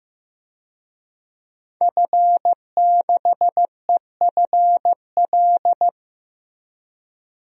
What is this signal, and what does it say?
Morse code, heard as a keyed tone, about 15 wpm, 700 Hz. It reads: F6EFL
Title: DXCC-100 Call Signs - 5 Characters - Repeated 2x - 15wpm